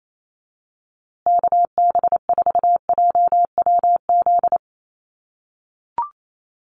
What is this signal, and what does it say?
Morse code, heard as a keyed tone, about 28 wpm, 700 Hz. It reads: KB4JWZ
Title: USA Call Signs Repeated 2x 28wpm